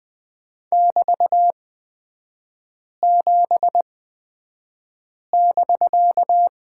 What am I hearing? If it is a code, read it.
Morse code, heard as a keyed tone, about 20 wpm, spaced as for 5 wpm, 700 Hz. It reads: <BT>7<BK>
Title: 250 Lesson 40e - BK - Three Characters 20wpm